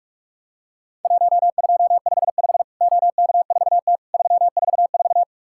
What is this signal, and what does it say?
Morse code, heard as a keyed tone, about 44 wpm, 700 Hz. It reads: 1255 OK4T 344
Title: Worked All Europe DX Contest - QTC Exchange - 45wpm